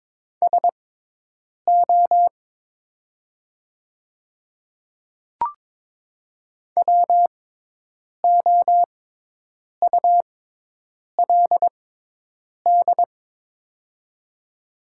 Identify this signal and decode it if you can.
Morse code, heard as a keyed tone, about 22 wpm, spaced as for 8 wpm, 700 Hz. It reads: SO WOULD
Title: 6x Character Spacing - Top 100 Words - 22wpm